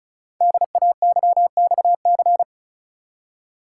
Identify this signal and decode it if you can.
Morse code, heard as a keyed tone, about 34 wpm, 700 Hz. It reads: DAYXC
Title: Code Groups - 5 Letters - 35wpm